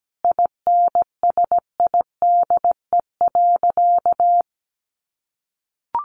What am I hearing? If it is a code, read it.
Morse code, heard as a keyed tone, about 17 wpm, 700 Hz. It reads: INSIDE.
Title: Easy Phrases 17wpm